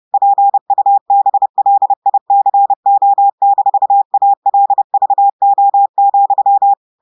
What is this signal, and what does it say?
Morse code, heard as a keyed tone, about 30 wpm, 800 Hz. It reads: PUBLICO-ALVO,
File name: audios / portugais / portugais texte en clair 2 vitesse 30 mots minutes